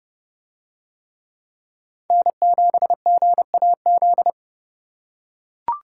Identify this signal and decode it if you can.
Morse code, heard as a keyed tone, about 31 wpm, 700 Hz. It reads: N7GAZ